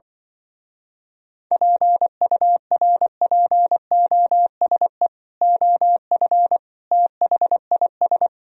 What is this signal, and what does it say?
Morse code, heard as a keyed tone, about 24 wpm, 700 Hz. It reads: PURPOSE OF THIS
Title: Speed-Racing - Sets of 5 Words 20wpm